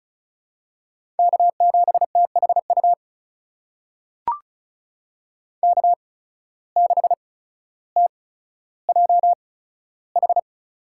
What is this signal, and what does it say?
Morse code, heard as a keyed tone, about 36 wpm, spaced as for 15 wpm, 700 Hz. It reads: K7THU K6TJH
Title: USA Call Signs - 5 Characters - 8x Character Spacing - 35wpm